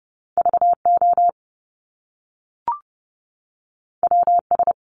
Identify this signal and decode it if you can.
Morse code, heard as a keyed tone, about 30 wpm, 700 Hz. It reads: VO WS
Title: Top Two Letter Combinations in Words 30wpm